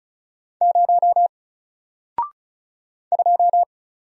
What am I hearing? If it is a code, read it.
Morse code, heard as a keyed tone, about 36 wpm, 700 Hz. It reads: 0 2
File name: Single-Letter-Number-35wpm